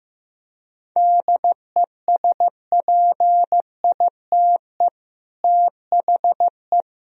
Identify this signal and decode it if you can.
Morse code, heard as a keyed tone, about 15 wpm, 700 Hz. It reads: DESPITE THE
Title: Speed-Racing - Sets of 4 Words 15wpm